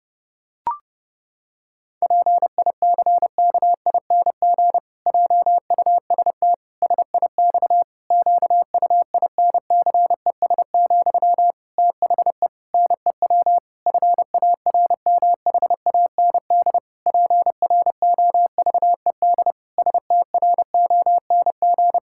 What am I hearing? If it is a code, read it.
Morse code, heard as a keyed tone, about 30 wpm, 700 Hz. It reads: PICKING JUST SIX QUINCES, THE NEW FARMHAND PROVED STRONG
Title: Pangram Challenge v2 - 30wpm